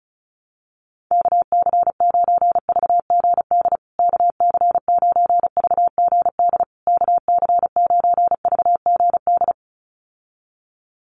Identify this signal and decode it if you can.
Morse code, heard as a keyed tone, about 34 wpm, 700 Hz. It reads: KC9VGD KC9VGD KC9VGD
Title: USA Call Signs Repeated 3x 35wpm